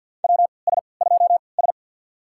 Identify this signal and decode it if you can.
Morse code, heard as a keyed tone, about 52 wpm, spaced as for 29 wpm, 700 Hz. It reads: WS2S
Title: USA Call Signs - 4 Characters - 3x Character Spacing - 50wpm